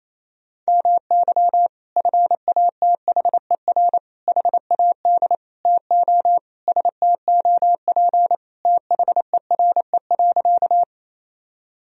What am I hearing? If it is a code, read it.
Morse code, heard as a keyed tone, about 28 wpm, 700 Hz. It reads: MY FATHER HAD TO STOP THERE.
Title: Sentences from Top 1000 Words - Review Part 1 - 28wpm